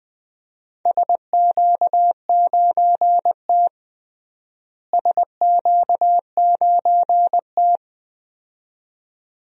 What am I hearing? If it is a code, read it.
Morse code, heard as a keyed tone, about 20 wpm, 700 Hz. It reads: SQ9T SQ9T
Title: DXCC-100 Call Signs - 4 Characters - Repeated 2x - 20wpm